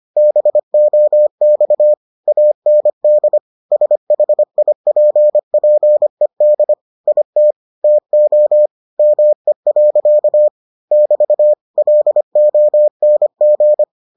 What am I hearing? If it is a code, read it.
Morse code, heard as a keyed tone, about 25 wpm, 600 Hz. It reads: BOX AND SHIPPED IT TO ME. <BT> LONG